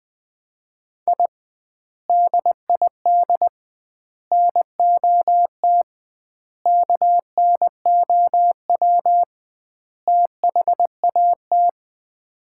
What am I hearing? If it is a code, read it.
Morse code, heard as a keyed tone, about 20 wpm, 700 Hz. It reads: I DID NOT KNOW THAT
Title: 2x Word-Spacing - Sets of 5 Words - 20wpm